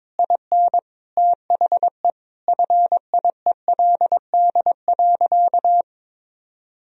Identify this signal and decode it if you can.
Morse code, heard as a keyed tone, about 22 wpm, 700 Hz. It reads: IN THE FIELD.
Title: Sentences from Top 1000 Words - Review Part 4 - 22wpm